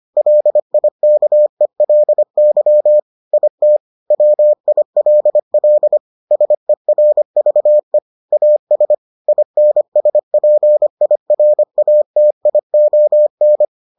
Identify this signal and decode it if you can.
Morse code, heard as a keyed tone, about 25 wpm, 600 Hz. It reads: LIKELY IT WILL SERVE AS INSPIRATION